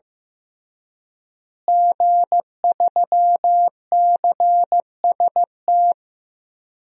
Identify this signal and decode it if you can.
Morse code, heard as a keyed tone, about 16 wpm, 700 Hz. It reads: G3CST
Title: DXCC-100 Call Signs - 5 Characters - Repeated 3x - 15wpm